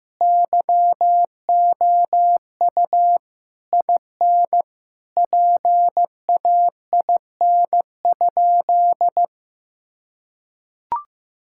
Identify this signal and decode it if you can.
Morse code, heard as a keyed tone, about 15 wpm, 700 Hz. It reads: YOU IN PAIN?
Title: Sentences from Top 1600 Words 15wpm